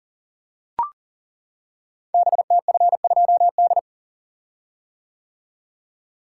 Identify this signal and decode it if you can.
Morse code, heard as a keyed tone, about 38 wpm, 700 Hz. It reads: BTF2D
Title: Code Groups - 5 Letter:Numbers - 40wpm